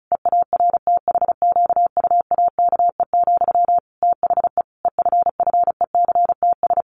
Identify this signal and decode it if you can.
Morse code, heard as a keyed tone, about 34 wpm, 700 Hz. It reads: EARTHQUAKE, THE EFFECTS
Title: A Sound Of Thunder 35wpm